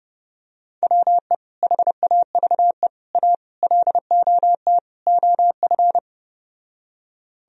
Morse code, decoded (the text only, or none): WE HAVE A LOT OF